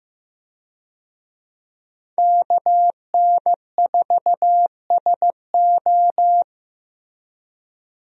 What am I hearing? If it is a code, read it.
Morse code, heard as a keyed tone, about 15 wpm, 700 Hz. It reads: KN4SO
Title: USA Call Signs - 5 Characters - Repeated 3x - 15wpm